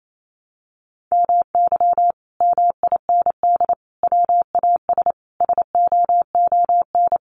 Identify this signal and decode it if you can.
Morse code, heard as a keyed tone, about 28 wpm, 700 Hz. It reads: MY MIND WAS SOON
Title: Sentences from Top 500 Words 28wpm